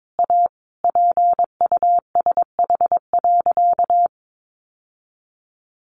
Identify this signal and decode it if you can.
Morse code, heard as a keyed tone, about 22 wpm, 700 Hz. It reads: A PUSH.
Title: Sentences from Top 1300 Words 22wpm